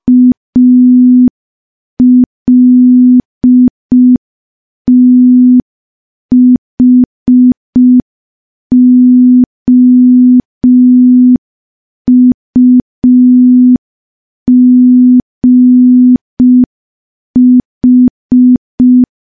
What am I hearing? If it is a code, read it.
Morse code, heard as a keyed tone, about 5 wpm, 250 Hz. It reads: ALTHOUGH
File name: morse / morse-training-05WPM